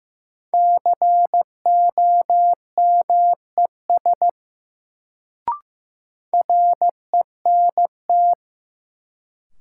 Morse code, heard as a keyed tone, about 15 wpm, 700 Hz. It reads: COMES RENT
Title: Sentences from Top 1000 Words - Review Part 5 - New Word Drill - 15wpm